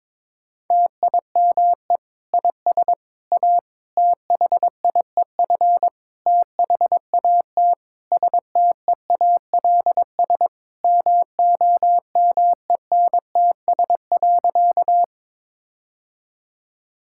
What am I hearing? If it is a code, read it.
Morse code, heard as a keyed tone, about 21 wpm, 700 Hz. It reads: TIME IS A THIEF THAT STEALS MOMENTS.
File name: Difficult Sentences with Hints - 22wpm